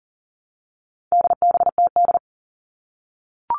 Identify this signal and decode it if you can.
Morse code, heard as a keyed tone, about 40 wpm, 700 Hz. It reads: DBTD
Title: All Cut Numbers - 4-Digit 40wpm